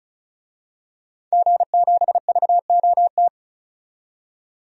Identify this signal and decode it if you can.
Morse code, heard as a keyed tone, about 36 wpm, 700 Hz. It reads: G7VOT
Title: DXCC-100 Call Signs - 5 Characters - 35wpm